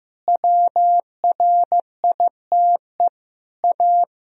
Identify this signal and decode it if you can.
Morse code, heard as a keyed tone, about 16 wpm, 700 Hz. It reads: WRITE A
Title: Sentences from Top 1000 Words - Review Part 3 - 15wpm